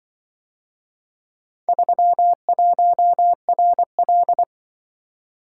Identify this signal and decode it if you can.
Morse code, heard as a keyed tone, about 24 wpm, 700 Hz. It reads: 31RL
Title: Speed-Racing - Four Letter-Number Combinations - 20wpm